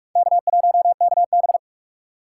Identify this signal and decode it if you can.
Morse code, heard as a keyed tone, about 48 wpm, 700 Hz. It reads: K1KB
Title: USA Call Signs - 4 Characters - 2x Character Spacing - 45wpm